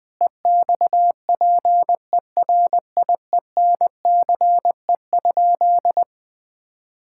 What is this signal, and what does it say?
Morse code, heard as a keyed tone, about 20 wpm, 700 Hz. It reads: EXPERIENCE?
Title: Sentences from Top 1600 Words 20wpm